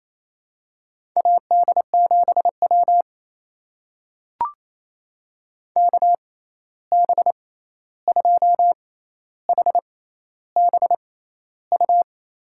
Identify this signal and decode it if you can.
Morse code, heard as a keyed tone, about 28 wpm, spaced as for 13 wpm, 700 Hz. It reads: AD7W KB2HBU
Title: USA Call Signs - 6x Character Spacing - 28wpm